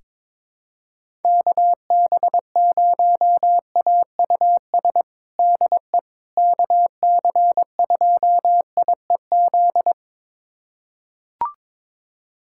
Morse code, heard as a keyed tone, about 22 wpm, 700 Hz. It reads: KB0AUS DE KC2IEZ
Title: USA Callsign DE Callsign 22wpm